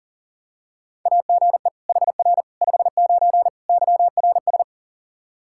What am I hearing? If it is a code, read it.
Morse code, heard as a keyed tone, about 40 wpm, 700 Hz. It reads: AGE HR 59 YRS
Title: Practice QSO - Exchange 3 - More Info - 40wpm